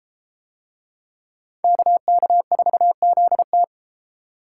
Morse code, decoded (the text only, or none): KK4ZT